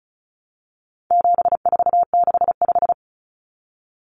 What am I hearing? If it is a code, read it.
Morse code, heard as a keyed tone, about 34 wpm, 700 Hz. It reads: 7465
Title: Numbers - 4-Digit 35wpm